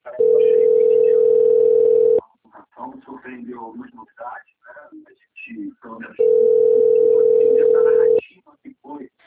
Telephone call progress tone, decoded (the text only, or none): ringback tone